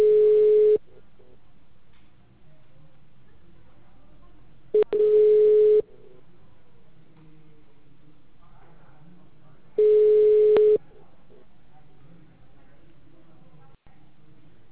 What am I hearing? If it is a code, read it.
Ringback tone (Europe, 425 Hz), with speech.